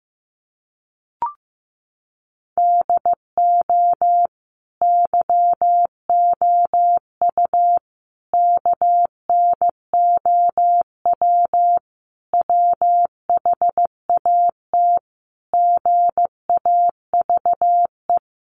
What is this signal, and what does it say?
Morse code, heard as a keyed tone, about 15 wpm, 700 Hz. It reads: DO YOU KNOW WHAT GAVE
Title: Sentences from Top 1300 Words 15wpm